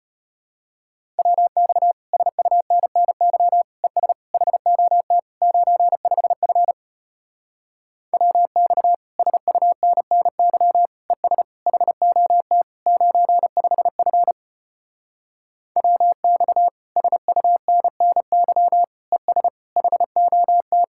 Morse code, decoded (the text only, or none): WX SUNNY ES HOT 95F WX SUNNY ES HOT 95F WX SUNNY ES HOT